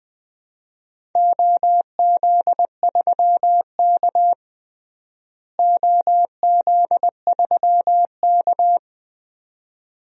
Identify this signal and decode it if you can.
Morse code, heard as a keyed tone, about 20 wpm, 700 Hz. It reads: OZ3K OZ3K
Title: DXCC-100 Call Signs - 4 Characters - Repeated 2x - 20wpm